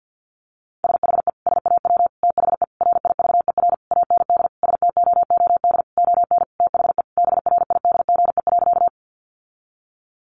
Morse code, heard as a keyed tone, about 50 wpm, 700 Hz. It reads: SHE SAW THE RIVER AND STOOD ON THE BRIDGE.
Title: Sentences from Top 1200 Words 50wpm